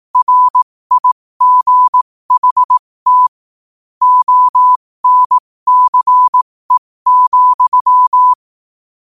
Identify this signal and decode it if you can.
Morse code, heard as a keyed tone, about 18 wpm, spaced as for 15 wpm, 1000 Hz. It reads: RIGHT ONCE,
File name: qotdfiles / 15wpm